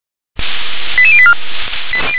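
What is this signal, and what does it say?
Five-tone selective call (ZVEI), heard as quick runs of five tones, tones of about 70 ms. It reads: 88084
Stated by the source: ZVEI five-tone selective call received on an SDR.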